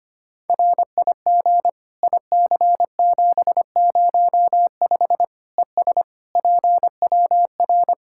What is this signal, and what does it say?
Morse code, heard as a keyed tone, about 25 wpm, 700 Hz. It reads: RIG IC705 ES PWR
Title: Practice QSO - Exchange 2 - Info - 25wpm